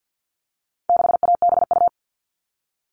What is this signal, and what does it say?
Morse code, heard as a keyed tone, about 50 wpm, 700 Hz. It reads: 6ABU
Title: All Cut Numbers - 4-Digit 50wpm